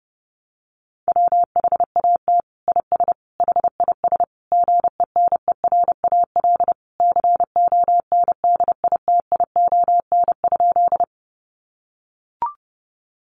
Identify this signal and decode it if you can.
Morse code, heard as a keyed tone, about 30 wpm, 700 Hz. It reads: WHAT IS HIS GENERAL CONDITION?
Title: Sentences from Top 1000 Words 30wpm